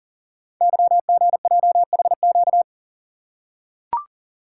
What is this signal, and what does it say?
Morse code, heard as a keyed tone, about 40 wpm, 700 Hz. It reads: YGJHQ
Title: Code Groups - 5 Letters - 40wpm